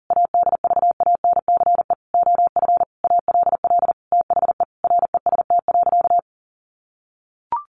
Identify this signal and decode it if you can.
Morse code, heard as a keyed tone, about 40 wpm, 700 Hz. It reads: ADVANCE OF ALL THE REST.
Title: Sentences from Top 1400 Words 40wpm